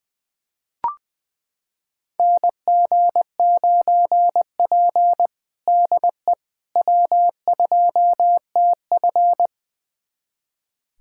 Morse code, heard as a keyed tone, about 20 wpm, 700 Hz. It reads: NG9P DE W2TF
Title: USA Callsign DE Callsign - 4 Characters - 20wpm